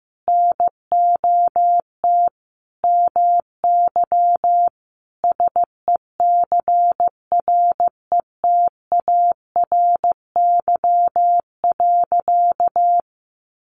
Morse code, heard as a keyed tone, about 15 wpm, 700 Hz. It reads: NOT MY SECRETARY.
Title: Sentences from Top 1300 Words 15wpm